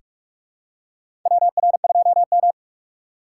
Morse code, heard as a keyed tone, about 45 wpm, 700 Hz. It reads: WR2M
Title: USA Call Signs - 4 Characters - 6x Character Spacing - 45wpm